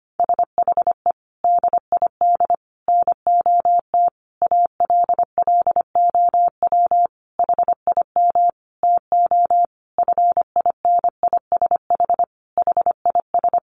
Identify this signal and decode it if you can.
Morse code, heard as a keyed tone, about 25 wpm, 700 Hz. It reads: SHE DID NOT ALLOW HIM TO FINISH HIS
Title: Sentences from Top 1200 Words 25wpm